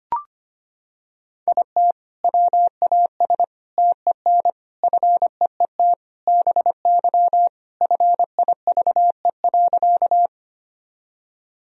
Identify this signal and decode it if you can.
Morse code, heard as a keyed tone, about 25 wpm, 700 Hz. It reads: IT WAS TEN FEET BY FIVE.
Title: Sentences from Top 1000 Words - Review Part 2 - 25wpm